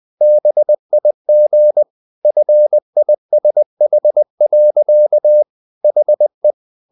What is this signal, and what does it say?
Morse code, heard as a keyed tone, about 20 wpm, 600 Hz. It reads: BIG FISH. HE
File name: the old man and the sea / the old man and the sea part 1 20wpm